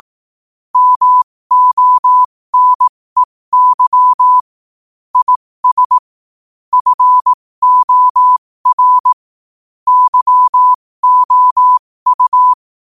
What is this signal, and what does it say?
Morse code, heard as a keyed tone, about 18 wpm, spaced as for 15 wpm, 1000 Hz. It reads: MONEY IS FOR YOU
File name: 15wpm